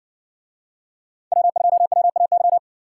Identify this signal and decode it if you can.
Morse code, heard as a keyed tone, about 60 wpm, 700 Hz. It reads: W2WAY